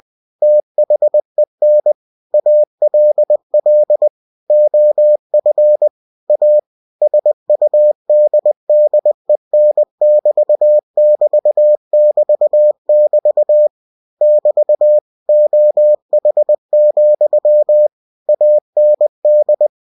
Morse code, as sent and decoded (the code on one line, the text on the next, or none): - .... -. / .- .-.. .-.. / --- ..-. / .- / ... ..- -.. -.. -. -...- -...- -...- -...- / -...- / --- .... --..-- / .- -. -..
THEN ALL OF A SUDDEN<BT><BT><BT><BT> <BT> OH, AND